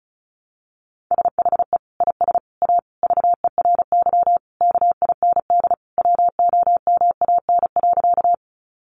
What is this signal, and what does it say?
Morse code, heard as a keyed tone, about 34 wpm, 700 Hz. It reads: SHE IS A VERY KIND WOMAN.
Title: Sentences from Top 1100 Words 35wpm